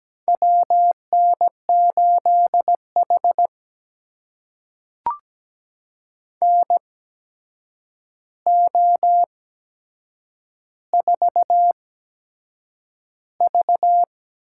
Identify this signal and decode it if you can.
Morse code, heard as a keyed tone, about 17 wpm, spaced as for 7 wpm, 700 Hz. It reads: WN8H NO4V
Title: USA Call Signs - 4 Characters - 8x Character Spacing - 17wpm